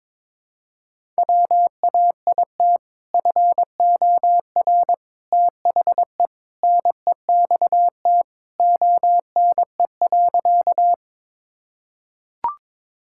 Morse code, heard as a keyed tone, about 22 wpm, 700 Hz. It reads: WAIT FOR THE NEXT ONE.
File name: practice / 22wpm / Sentences with 2-Word Phrases - 22wpm